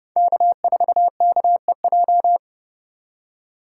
Morse code, heard as a keyed tone, about 32 wpm, 700 Hz. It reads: K4KEJ
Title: USA Call Signs - 5 Characters - 3x Character Spacing - 30wpm